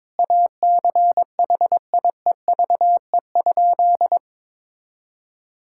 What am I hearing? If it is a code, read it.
Morse code, heard as a keyed tone, about 22 wpm, 700 Hz. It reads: ACHIEVE?
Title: Sentences from Top 1000 Words - Review Part 6 - 22wpm